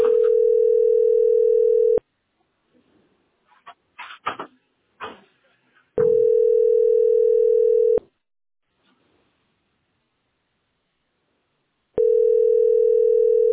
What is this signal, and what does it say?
Ringback tone (North America), 440 + 480 Hz.